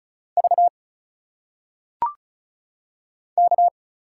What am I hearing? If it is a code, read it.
Morse code, heard as a keyed tone, about 34 wpm, 700 Hz. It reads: V K
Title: Single Letters - Phonetic Pronunciation 35wpm